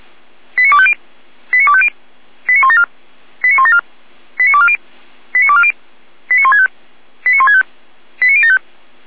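Five-tone selective call (ZVEI), heard as quick runs of five tones, tones of about 70 ms. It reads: 89250 89250 89174 89174 89240 89240 89165 89165 89075
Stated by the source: Five-tone ZVEI selective call received on SDR.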